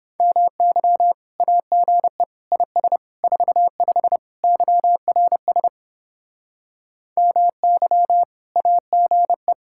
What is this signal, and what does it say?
Morse code, heard as a keyed tone, about 29 wpm, 700 Hz. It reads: MY AGE IS 45 YRS MY AGE